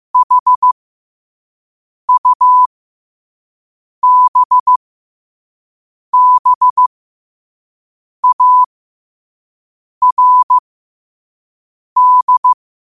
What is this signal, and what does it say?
Morse code, heard as a keyed tone, about 15 wpm, spaced as for 5 wpm, 1000 Hz. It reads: HUBBARD
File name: qotdfiles / 5wpm